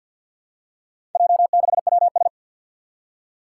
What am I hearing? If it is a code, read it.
Morse code, heard as a keyed tone, about 50 wpm, 700 Hz. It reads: J6WS
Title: Four Challenging Letter-Number Combinations 50wpm